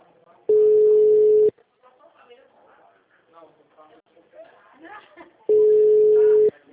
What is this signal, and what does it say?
Ringback tone (Europe, 425 Hz), with speech.